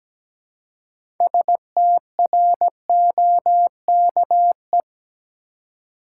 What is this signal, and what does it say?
Morse code, heard as a keyed tone, about 17 wpm, 700 Hz. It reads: STROKE
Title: Six Letter Words - 17wpm